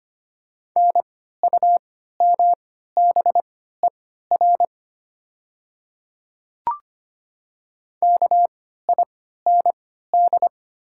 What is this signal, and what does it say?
Morse code, heard as a keyed tone, about 25 wpm, spaced as for 14 wpm, 700 Hz. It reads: NUMBER KIND
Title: 3x Character Spacing - Top 200 Words - 25wpm